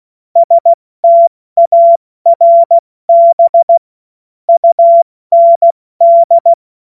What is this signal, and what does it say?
Morse code, heard as a keyed tone, about 16 wpm, 680 Hz. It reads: STARB UND
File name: Morsekurs / Witze / Witz 25. Oktober 2022 70 F80